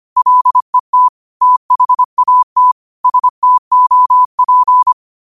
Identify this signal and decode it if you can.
Morse code, heard as a keyed tone, about 25 wpm, 1000 Hz. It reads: LET THAT STOP